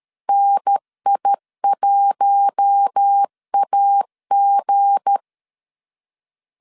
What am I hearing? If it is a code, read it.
Morse code, heard as a keyed tone, about 13 wpm, 800 Hz. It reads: NI1AG
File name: LEZIONI / 7 GF9Z / 7 GF9 12wpm 2319